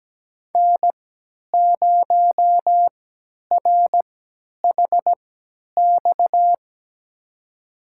Morse code, heard as a keyed tone, about 17 wpm, spaced as for 10 wpm, 700 Hz. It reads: N0RHX
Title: USA Call Signs - 5 Characters - 3x Character Spacing - 17wpm